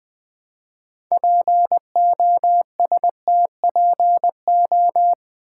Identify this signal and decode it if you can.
Morse code, heard as a keyed tone, about 20 wpm, 700 Hz. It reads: POSTPO
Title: Build A Word - 3 Letters At A Time - 8-Letter Words - 20wpm